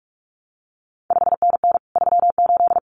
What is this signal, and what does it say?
Morse code, heard as a keyed tone, about 46 wpm, 700 Hz. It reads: 5NN 38
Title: CQ WW DX Contest Exchange Encore - 45wpm